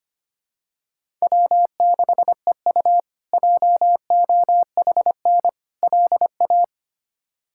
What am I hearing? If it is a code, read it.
Morse code, heard as a keyed tone, about 25 wpm, 700 Hz. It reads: W6EU JOHN LA
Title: K1USN SST QSO Practice - 25wpm